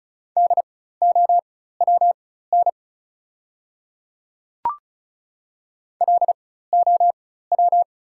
Morse code, heard as a keyed tone, about 36 wpm, spaced as for 16 wpm, 700 Hz. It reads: DOWN LOW
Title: 4x Character Spacing - Top 200 Words - 35wpm